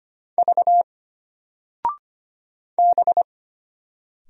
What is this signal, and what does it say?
Morse code, heard as a keyed tone, about 25 wpm, 700 Hz. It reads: V B
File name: Single-Letter-Number-25wpm